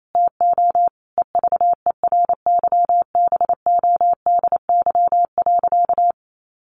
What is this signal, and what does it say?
Morse code, heard as a keyed tone, about 28 wpm, 700 Hz. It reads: TO EVERYBODY.
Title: Sentences from Top 1400 Words 28wpm